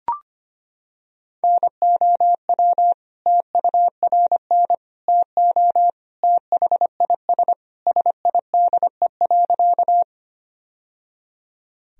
Morse code, heard as a keyed tone, about 25 wpm, 700 Hz. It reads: NOW TURN TO THIS SIDE.
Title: Sentences from Top 1000 Words - Review Part 1 - 25wpm